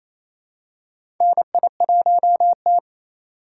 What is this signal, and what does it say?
Morse code, heard as a keyed tone, about 28 wpm, 700 Hz. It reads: NI1T
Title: USA Call Signs - 4x Character Spacing - 28wpm